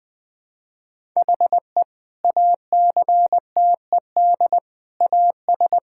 Morse code, heard as a keyed tone, about 19 wpm, 700 Hz. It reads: HE ACTED AS
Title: Sentences with 5-Word Phrases - 20wpm